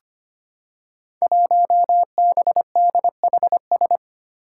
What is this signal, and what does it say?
Morse code, heard as a keyed tone, about 25 wpm, 700 Hz. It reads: 1BDHS
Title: Five Challenging Letter-Number Combinations 25wpm